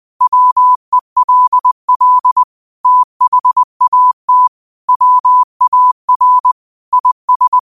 Morse code, heard as a keyed tone, about 20 wpm, 1000 Hz. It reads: WELL THAT WAR IS
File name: qotdfiles / 20wpm